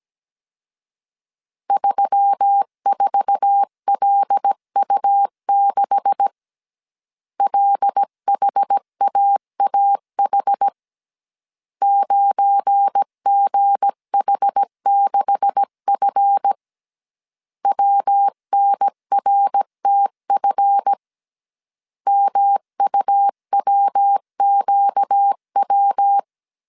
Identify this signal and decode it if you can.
Morse code, heard as a keyed tone, about 17 wpm, 800 Hz. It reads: 34LU6 LHAAH 9GH6F WNRTF MUWQW
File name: 1500C S B 16wpm